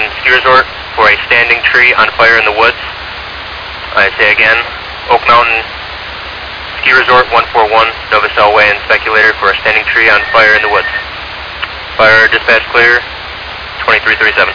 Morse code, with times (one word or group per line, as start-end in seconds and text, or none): none